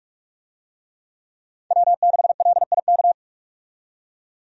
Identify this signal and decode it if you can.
Morse code, heard as a keyed tone, about 41 wpm, 700 Hz. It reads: W6LIK
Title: USA Call Signs - 5 Characters - 45wpm